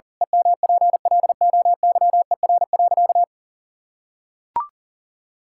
Morse code, heard as a keyed tone, about 40 wpm, 700 Hz. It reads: EMPLOYER.
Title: Pangram Challenge v2 - 40wpm